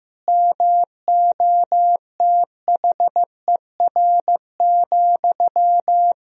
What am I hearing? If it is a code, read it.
Morse code, heard as a keyed tone, about 16 wpm, 700 Hz. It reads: MOTHER,
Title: Sentences from Top 1000 Words - Review Part 3 - 15wpm